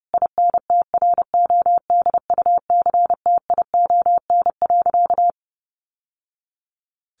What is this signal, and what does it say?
Morse code, heard as a keyed tone, about 30 wpm, 700 Hz. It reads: INTRODUCTION.
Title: Sentences from Top 1000 Words 30wpm